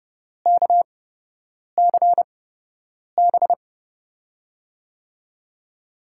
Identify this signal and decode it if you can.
Morse code, heard as a keyed tone, about 30 wpm, spaced as for 8 wpm, 700 Hz. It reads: KCB
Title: USA Call Signs - 6 Characters - 8x Character Spacing - 30wpm